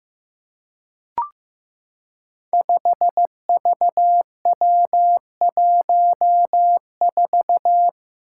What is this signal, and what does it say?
Morse code, heard as a keyed tone, about 15 wpm, 700 Hz. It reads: 5VW14
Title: Five Challenging Letter-Number Combinations 15wpm